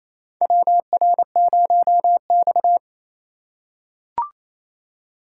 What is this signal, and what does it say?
Morse code, heard as a keyed tone, about 28 wpm, 700 Hz. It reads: WR0X
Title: USA Call Signs - 4x Character Spacing - 28wpm